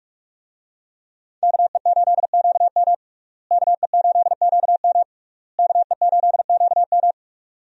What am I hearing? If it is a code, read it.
Morse code, heard as a keyed tone, about 45 wpm, 700 Hz. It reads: KE8QM KE8QM KE8QM
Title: USA Call Signs - 5 Characters - Repeated 3x - 45wpm